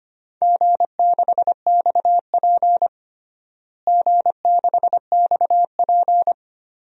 Morse code, heard as a keyed tone, about 25 wpm, 700 Hz. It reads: G6XP G6XP
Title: DXCC-100 Call Signs - 4 Characters - Repeated 3x - 25wpm